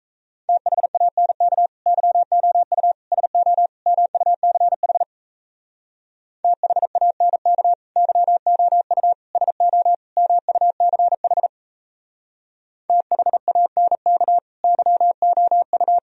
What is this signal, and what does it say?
Morse code, heard as a keyed tone, about 35 wpm, 700 Hz. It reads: THANK YOU SO MUCH THANK YOU SO MUCH THANK YOU
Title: Speed-Racing - Sets of 4 Words 28wpm